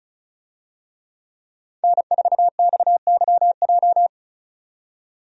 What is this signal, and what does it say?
Morse code, heard as a keyed tone, about 36 wpm, 700 Hz. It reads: N4XYJ
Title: USA Call Signs - 5 Characters - 35wpm